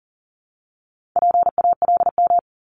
Morse code, heard as a keyed tone, about 41 wpm, 700 Hz. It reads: PALM
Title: Multi-letter Phonemes 40wpm